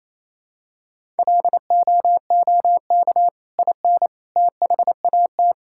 Morse code, heard as a keyed tone, about 27 wpm, 700 Hz. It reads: LOOK IN THAT
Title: Sentences from Top 1000 Words - Review Part 5 - 28wpm